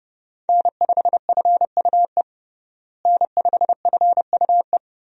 Morse code, heard as a keyed tone, about 30 wpm, 700 Hz. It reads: N5FUE N5FUE
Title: USA Call Signs - 5 Characters - Repeated 2x - 30wpm